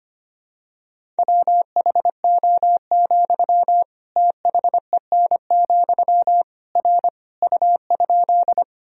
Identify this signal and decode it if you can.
Morse code, heard as a keyed tone, about 25 wpm, 700 Hz. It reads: WHO, THEN, R U?
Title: Sentences from Top 100 Words and Abbreviations 25wpm